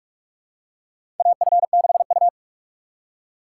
Morse code, heard as a keyed tone, about 46 wpm, 700 Hz. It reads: AF6U
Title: USA Call Signs - 4 Characters - 3x Character Spacing - 45wpm